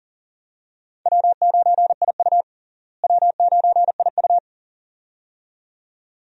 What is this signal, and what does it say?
Morse code, heard as a keyed tone, about 40 wpm, 700 Hz. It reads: W9IU W9IU
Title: USA Call Signs - 4 Characters - Repeated 2x - 40wpm